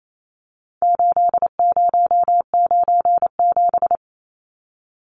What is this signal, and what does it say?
Morse code, heard as a keyed tone, about 28 wpm, 700 Hz. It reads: 8097
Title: Numbers - 4-Digit 28wpm